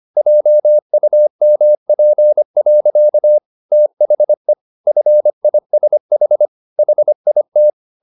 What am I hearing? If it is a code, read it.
Morse code, heard as a keyed tone, about 25 wpm, 600 Hz. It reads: JUMP. THE FISH HIT